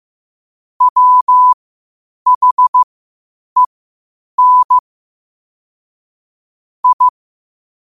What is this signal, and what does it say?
Morse code, heard as a keyed tone, about 15 wpm, spaced as for 8 wpm, 1000 Hz. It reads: WHEN I
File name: qotdfiles / 8wpm